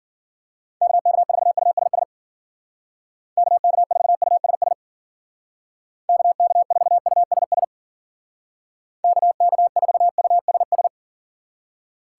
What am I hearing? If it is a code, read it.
Morse code, heard as a keyed tone, about 52 wpm, 700 Hz. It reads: KK4USS KK4USS KK4USS KK4USS